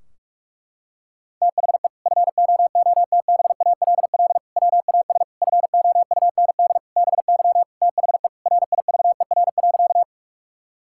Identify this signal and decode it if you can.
Morse code, heard as a keyed tone, about 46 wpm, 700 Hz. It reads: THE FOOTBALL WAS FOUND BY THE RIVER.
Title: Sentences from Top 1600 Words 45wpm